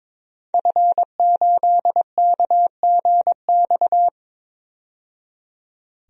Morse code, heard as a keyed tone, about 22 wpm, 700 Hz. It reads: F8KGX
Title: DXCC-100 Call Signs - 5 Characters - 22wpm